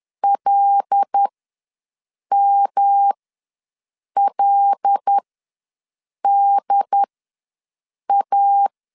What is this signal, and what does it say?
Morse code, heard as a keyed tone, about 10 wpm, spaced as for 6 wpm, 800 Hz. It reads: LMLDA
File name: LEZIONI / 5 LBVC / 5 L ws 10wpm 3636